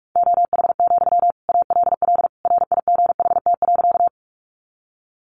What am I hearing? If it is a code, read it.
Morse code, heard as a keyed tone, about 44 wpm, 700 Hz. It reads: OH, ALL RIGHT.
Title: Sentences from Top 500 Words 45wpm